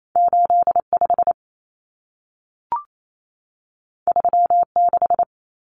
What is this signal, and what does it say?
Morse code, heard as a keyed tone, about 28 wpm, 700 Hz. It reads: 85 36
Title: Common Cut Numbers - 2-Digit 28wpm